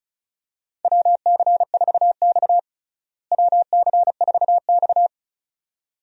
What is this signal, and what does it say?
Morse code, heard as a keyed tone, about 36 wpm, 700 Hz. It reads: WC4X WC4X
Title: USA Call Signs - 4 Characters - Repeated 2x - 35wpm